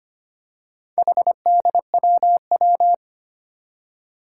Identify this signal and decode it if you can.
Morse code, heard as a keyed tone, about 25 wpm, 700 Hz. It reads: HDWW
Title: Four Challenging Letter-Number Combinations 25wpm